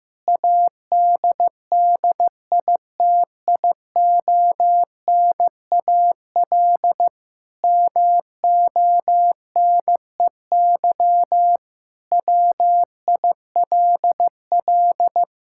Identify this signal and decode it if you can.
Morse code, heard as a keyed tone, about 15 wpm, 700 Hz. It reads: ADDITIONAL MONEY WILL
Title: Sentences from Top 1000 Words - Review Part 5 - 15wpm